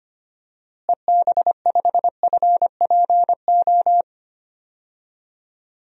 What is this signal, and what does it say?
Morse code, heard as a keyed tone, about 25 wpm, 700 Hz. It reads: EB5FPO
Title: DXCC-100 Call Signs - 6 Characters - Repeated 2x - 25wpm